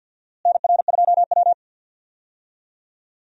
Morse code, heard as a keyed tone, about 52 wpm, 700 Hz. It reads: NR2W
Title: USA Call Signs - 4 Characters - Repeated 3x - 50wpm